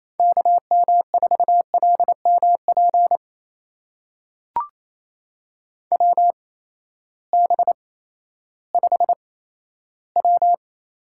Morse code, heard as a keyed tone, about 28 wpm, spaced as for 14 wpm, 700 Hz. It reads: KM4LMP WB5W
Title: USA Call Signs - 8x Character Spacing - 28wpm